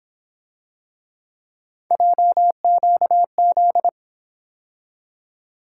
Morse code, heard as a keyed tone, about 27 wpm, 700 Hz. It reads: JQZ
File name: Speed-Racing - Three Letter-Number Combinations - 22wpm